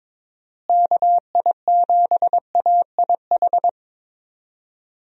Morse code, heard as a keyed tone, about 22 wpm, 700 Hz. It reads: KI7AIH
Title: USA Call Signs - 6 Characters - Repeated 3x - 22wpm